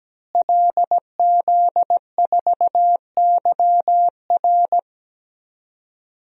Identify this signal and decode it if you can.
Morse code, heard as a keyed tone, about 17 wpm, 700 Hz. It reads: LZ4YR
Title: DXCC-100 Call Signs - 5 Characters - Repeated 3x - 17wpm